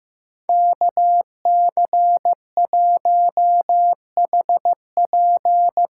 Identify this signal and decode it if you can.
Morse code, heard as a keyed tone, about 15 wpm, 700 Hz. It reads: KC1HP
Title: USA Call Signs - 6 Characters - 8x Character Spacing - 15wpm